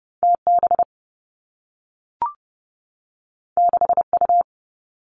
Morse code, heard as a keyed tone, about 30 wpm, 700 Hz. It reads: T B 6 U
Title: Two Challenging Letter-Number Combinations 30wpm